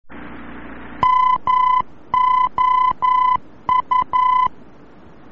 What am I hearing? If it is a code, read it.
Morse code, heard as a keyed tone, about 11 wpm, 1010 Hz. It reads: MOU
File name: france / mx / MOU-VOR-F